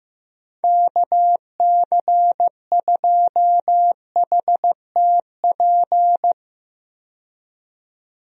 Morse code, heard as a keyed tone, about 15 wpm, 700 Hz. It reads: KC2HTP
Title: USA Call Signs - 6 Characters - 15wpm